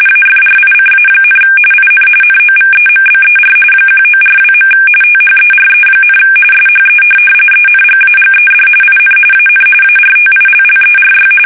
RTTY, about 45.45 baud. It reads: AFRICA, IN NOVEMBER 1942. THERE, SHE ENGAGED IN AN ARTILLERY
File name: fsk850 / USS-Massachusetts